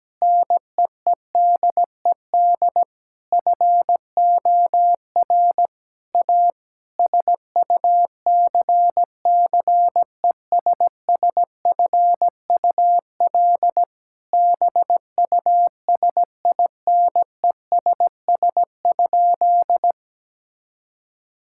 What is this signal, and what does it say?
Morse code, heard as a keyed tone, about 17 wpm, 700 Hz. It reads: NEEDED FOR A SUCCESSFUL BUSINESS?
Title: Sentences from Top 1000 Words - Review Part 6 - 17wpm